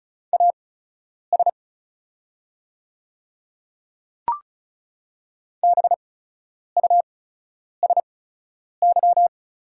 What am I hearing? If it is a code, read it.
Morse code, heard as a keyed tone, about 36 wpm, spaced as for 10 wpm, 700 Hz. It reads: AS BUSY
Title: 8x Character Spacing - Top 500 Words - 35wpm